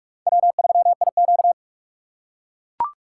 Morse code, heard as a keyed tone, about 44 wpm, 700 Hz. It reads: W3IQ
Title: USA Call Signs - 4 Characters - 2x Character Spacing - 45wpm